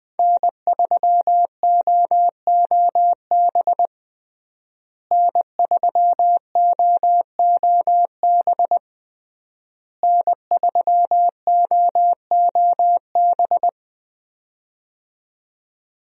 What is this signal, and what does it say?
Morse code, heard as a keyed tone, about 21 wpm, 700 Hz. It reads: N3OOB N3OOB N3OOB
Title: USA Call Signs - 5 Characters - Repeated 3x - 20wpm